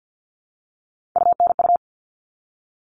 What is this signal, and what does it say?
Morse code, heard as a keyed tone, about 48 wpm, 700 Hz. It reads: UNU
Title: All Cut Numbers - 3-Digit 50wpm